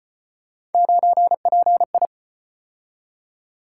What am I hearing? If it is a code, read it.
Morse code, heard as a keyed tone, about 34 wpm, 700 Hz. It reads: 9PI